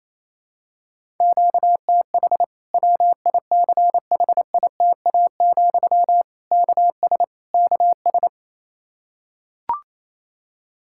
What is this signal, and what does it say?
Morse code, heard as a keyed tone, about 28 wpm, 700 Hz. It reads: QTH WICHITA, KS KS
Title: Practice QSO - Exchange 1 - Beginning - 28wpm